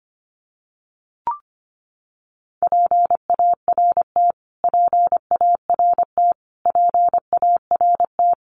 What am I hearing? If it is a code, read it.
Morse code, heard as a keyed tone, about 25 wpm, 700 Hz. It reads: PART PART PART
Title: Top 500 Words Repeated 25wpm